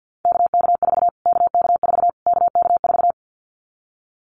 Morse code, heard as a keyed tone, about 50 wpm, 700 Hz. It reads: KK4 KK4 KK4
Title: Callsign Prefixes for Top 25 DXCC - Repeated 3x - 50wpm